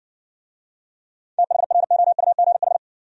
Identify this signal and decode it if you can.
Morse code, heard as a keyed tone, about 58 wpm, 700 Hz. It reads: THROUGH